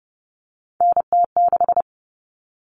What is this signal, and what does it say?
Morse code, heard as a keyed tone, about 30 wpm, 700 Hz. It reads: NT6
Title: All Cut Numbers - 3-Digit 30wpm